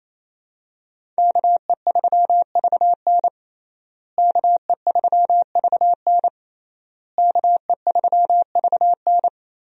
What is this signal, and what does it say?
Morse code, heard as a keyed tone, about 27 wpm, 700 Hz. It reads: KE3VN KE3VN KE3VN
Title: USA Call Signs - 5 Characters - Repeated 3x - 28wpm